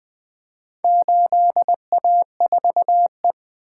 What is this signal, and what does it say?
Morse code, heard as a keyed tone, about 20 wpm, 700 Hz. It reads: 8A4E
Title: Four Letter-Number Combinations - 20wpm